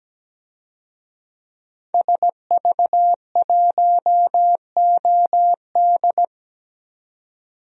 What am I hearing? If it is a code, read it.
Morse code, heard as a keyed tone, about 17 wpm, 700 Hz. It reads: SV1OD
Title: DXCC-100 Call Signs - 5 Characters - 17wpm